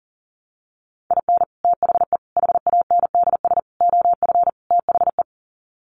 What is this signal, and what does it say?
Morse code, heard as a keyed tone, about 40 wpm, 700 Hz. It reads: IN THE HANDS OF THE
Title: Sets of 5 Words 40wpm